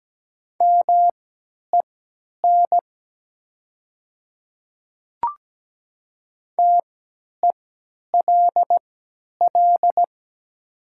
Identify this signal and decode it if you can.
Morse code, heard as a keyed tone, about 17 wpm, spaced as for 10 wpm, 700 Hz. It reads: MEN TELL
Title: 3x Character Spacing - Top 300 Words - 17wpm